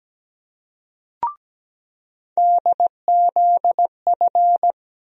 Morse code, heard as a keyed tone, about 17 wpm, 700 Hz. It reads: DZF
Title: USA Call Sign Postfixes - 3 Letters 17wpm